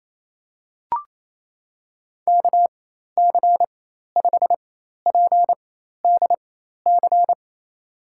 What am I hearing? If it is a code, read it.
Morse code, heard as a keyed tone, about 28 wpm, spaced as for 13 wpm, 700 Hz. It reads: KC5PDC
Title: USA Call Signs - 4x Character Spacing - 28wpm